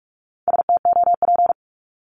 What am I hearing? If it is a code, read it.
Morse code, heard as a keyed tone, about 44 wpm, 700 Hz. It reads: STOP
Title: Top 500 Words 45wpm